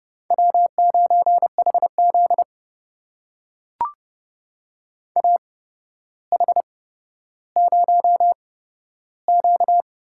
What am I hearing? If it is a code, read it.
Morse code, heard as a keyed tone, about 30 wpm, spaced as for 13 wpm, 700 Hz. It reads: W9HZ AH0Q
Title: USA Call Signs - 4 Characters - 8x Character Spacing - 30wpm